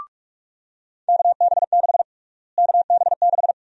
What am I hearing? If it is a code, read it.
Morse code, heard as a keyed tone, about 46 wpm, 700 Hz. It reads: KB6 KB6
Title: USA Call Sign Prefixes - Repeated 2x - 45wpm